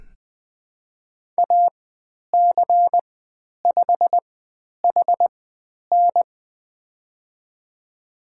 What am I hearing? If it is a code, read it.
Morse code, heard as a keyed tone, about 20 wpm, spaced as for 10 wpm, 700 Hz. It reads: AC5HN
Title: 109 Lesson 18g - Letter m - Call Signs 20wpm